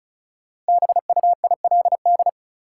Morse code, heard as a keyed tone, about 34 wpm, 700 Hz. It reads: BUILD
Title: Sentences from Top 1100 Words - New Word Drill Alt - 35wpm